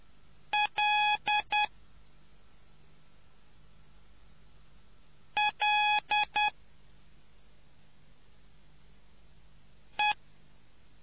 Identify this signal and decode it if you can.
Morse code, heard as a keyed tone, about 10 wpm, 840 Hz. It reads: L L E